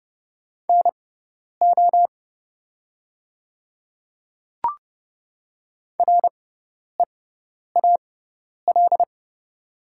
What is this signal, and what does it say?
Morse code, heard as a keyed tone, about 30 wpm, spaced as for 10 wpm, 690 Hz. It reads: NO REAL
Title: 6x Character Spacing - Top 300 Words - 30wpm